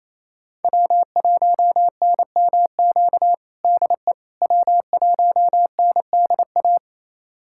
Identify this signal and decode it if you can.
Morse code, heard as a keyed tone, about 28 wpm, 700 Hz. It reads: W1NMQ DE W1NDA
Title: USA Callsign DE Callsign 28wpm